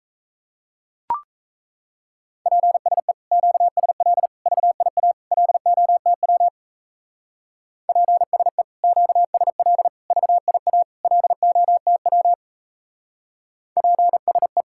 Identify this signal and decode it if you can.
Morse code, heard as a keyed tone, about 34 wpm, 700 Hz. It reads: PSE QSL VIA LOTW PSE QSL VIA LOTW PSE
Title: Speed-Racing Common QSO Phrases - 28wpm